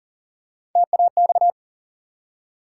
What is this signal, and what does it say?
Morse code, heard as a keyed tone, about 41 wpm, 700 Hz. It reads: TAX
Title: Three Letter Words - 40wpm